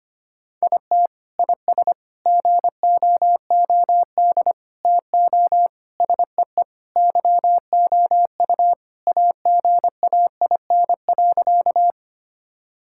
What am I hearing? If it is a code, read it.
Morse code, heard as a keyed tone, about 25 wpm, 700 Hz. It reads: IT IS GOOD TO SEE YOU AGAIN.
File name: practice / 25wpm / Sentences with 4-Word Phrases - 25wpm